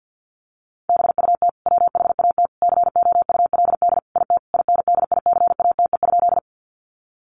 Morse code, heard as a keyed tone, about 50 wpm, 700 Hz. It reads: BUT WHAT COULD IT INDICATE?
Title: Sentences from Top 1300 Words 50wpm